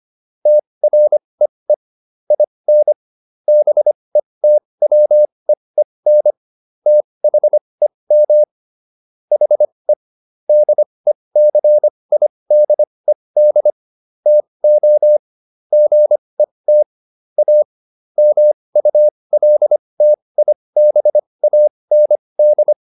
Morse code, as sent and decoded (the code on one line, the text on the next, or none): - .-. . / .. -. / -... - .-- . -. / - .... -- / .... / -.. -.-. .. -.. -.. / - --- / --. - / .- / -- ..- .-.. - .. -... .- -. -..
TREE IN BETWEEN THEM HE DECIDED TO GET A MULTIBAND